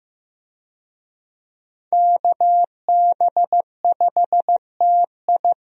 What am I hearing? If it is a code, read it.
Morse code, heard as a keyed tone, about 15 wpm, 700 Hz. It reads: KB5TI
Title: USA Call Signs - 5 Characters - Repeated 2x - 15wpm